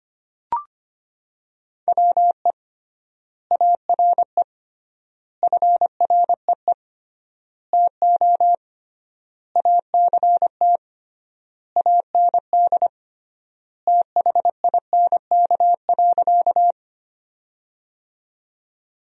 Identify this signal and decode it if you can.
Morse code, heard as a keyed tone, about 25 wpm, 700 Hz. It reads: WE ARE FREE TO ACT AND THINK.
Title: 3x Word-Spacing - Sentences from Top 500 Words - 25wpm